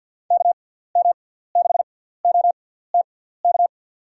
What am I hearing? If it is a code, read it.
Morse code, heard as a keyed tone, about 52 wpm, spaced as for 17 wpm, 700 Hz. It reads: KM6OTK
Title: USA Call Signs - 6 Characters - 6x Character Spacing - 50wpm